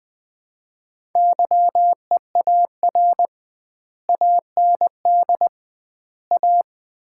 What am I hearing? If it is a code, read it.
Morse code, heard as a keyed tone, about 20 wpm, 700 Hz. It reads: YEAR AND A
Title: 2x Word-Spacing - Sets of 4 Words - 20wpm